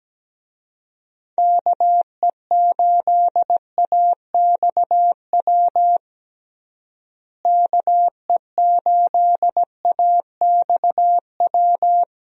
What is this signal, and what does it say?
Morse code, heard as a keyed tone, about 17 wpm, 700 Hz. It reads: KE8AXW KE8AXW
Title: USA Call Signs - 6 Characters - Repeated 3x - 17wpm